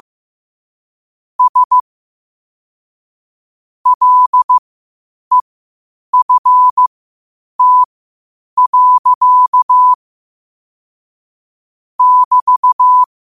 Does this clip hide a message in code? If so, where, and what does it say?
Yes: Morse code from 1.4 to 13.0 s, heard as a keyed tone, about 15 wpm, spaced as for 8 wpm, 1000 Hz. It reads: S LEFT. <BT>